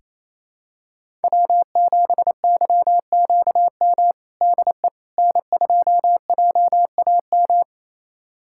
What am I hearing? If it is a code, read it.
Morse code, heard as a keyed tone, about 28 wpm, 700 Hz. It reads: W7YQM DE N2JAM
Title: USA Callsign DE Callsign - 5 Characters - 28wpm